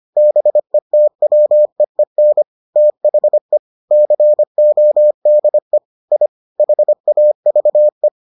Morse code, heard as a keyed tone, about 25 wpm, 600 Hz. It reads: BETWEEN THE CODE I HAVE